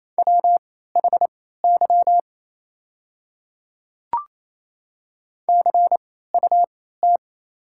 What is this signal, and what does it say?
Morse code, heard as a keyed tone, about 28 wpm, spaced as for 16 wpm, 700 Hz. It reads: WHY CUT
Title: 3x Character Spacing - Top 300 Words - 28wpm